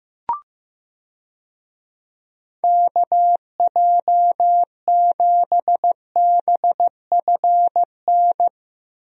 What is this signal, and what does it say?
Morse code, heard as a keyed tone, about 15 wpm, 700 Hz. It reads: KJ7BFN
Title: USA Call Signs - 6 Characters - Repeated 3x - 15wpm